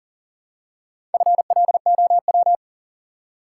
Morse code, heard as a keyed tone, about 40 wpm, 700 Hz. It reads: FLOW
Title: Four Letter Words - 40wpm